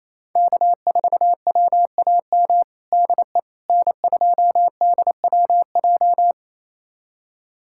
Morse code, heard as a keyed tone, about 28 wpm, 700 Hz. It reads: K4WAM DE N2DWJ
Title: USA Callsign DE Callsign - 5 Characters - 28wpm